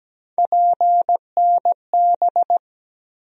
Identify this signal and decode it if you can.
Morse code, heard as a keyed tone, about 17 wpm, 700 Hz. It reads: PNB
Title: USA Call Sign Postfixes - 3 Letters 17wpm